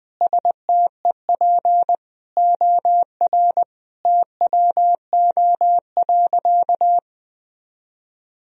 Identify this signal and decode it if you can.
Morse code, heard as a keyed tone, about 20 wpm, 700 Hz. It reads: STEP OR TWO.
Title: Sentences from Top 1000 Words - Review Part 2 - 20wpm